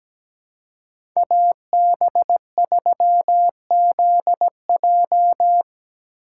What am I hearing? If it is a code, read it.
Morse code, heard as a keyed tone, about 17 wpm, 700 Hz. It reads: AB3ZJ
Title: USA Call Signs - 5 Characters - 4x Character Spacing - 17wpm